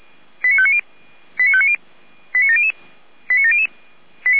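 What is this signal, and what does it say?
Five-tone selective call (ZVEI), heard as quick runs of five tones, tones of about 70 ms. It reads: 89590 89590 89700 89700
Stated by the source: ZVEI five-tone selective call received on an SDR.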